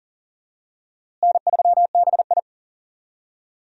Morse code, heard as a keyed tone, about 40 wpm, 700 Hz. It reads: N3BI
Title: USA Call Signs - 4 Characters - Repeated 2x - 40wpm